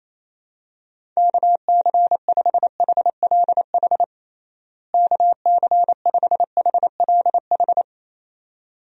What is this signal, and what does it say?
Morse code, heard as a keyed tone, about 28 wpm, 700 Hz. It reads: KC5HLH KC5HLH
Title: USA Call Signs - 6 Characters - Repeated 2x - 28wpm